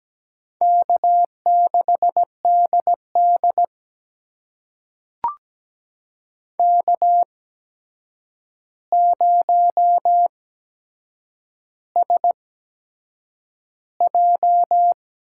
Morse code, heard as a keyed tone, about 17 wpm, spaced as for 7 wpm, 700 Hz. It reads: K6DD K0SJ
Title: USA Call Signs - 4 Characters - 8x Character Spacing - 17wpm